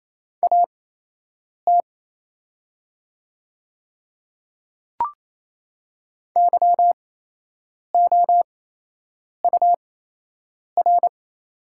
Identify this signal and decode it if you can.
Morse code, heard as a keyed tone, about 28 wpm, spaced as for 8 wpm, 700 Hz. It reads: AT YOUR
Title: 8x Character Spacing - Top 100 Words - 28wpm